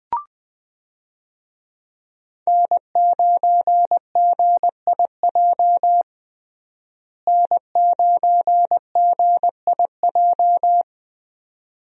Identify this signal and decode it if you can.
Morse code, heard as a keyed tone, about 21 wpm, 700 Hz. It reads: N9GIJ N9GIJ
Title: USA Call Signs - 5 Characters - Repeated 2x - 20wpm